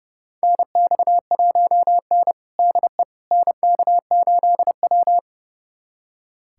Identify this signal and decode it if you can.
Morse code, heard as a keyed tone, about 32 wpm, 700 Hz. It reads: NX1N DE NK8W
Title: USA Callsign DE Callsign - 4 Characters - 30wpm